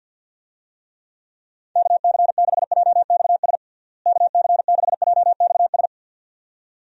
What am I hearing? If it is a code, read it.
Morse code, heard as a keyed tone, about 50 wpm, 700 Hz. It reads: KC6JXS KC6JXS
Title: USA Call Signs - 6 Characters - Repeated 2x - 50wpm